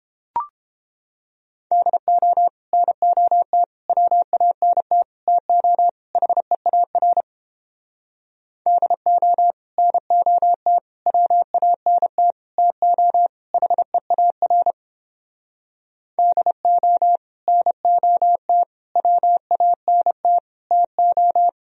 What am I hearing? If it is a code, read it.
Morse code, heard as a keyed tone, about 30 wpm, 700 Hz. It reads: DO NOT WANT TO HEAR DO NOT WANT TO HEAR DO NOT WANT TO